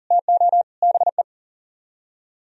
Morse code, heard as a keyed tone, about 51 wpm, 700 Hz. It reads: TO BE